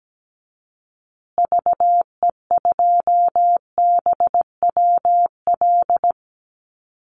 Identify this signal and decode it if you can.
Morse code, heard as a keyed tone, about 17 wpm, 700 Hz. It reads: VE2BWL
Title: Call Signs Repeated 2x - 17wpm